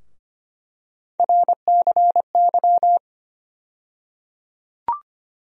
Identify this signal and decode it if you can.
Morse code, heard as a keyed tone, about 25 wpm, 700 Hz. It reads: RCY
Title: USA Call Sign Postfixes - 3 Letters 25wpm